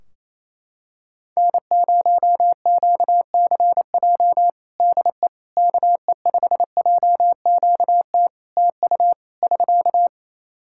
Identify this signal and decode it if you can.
Morse code, heard as a keyed tone, about 28 wpm, 700 Hz. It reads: N0QCJ DE KE5JQT TU <SK>
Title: Practice QSO - Exchange 4 - Ending - 28wpm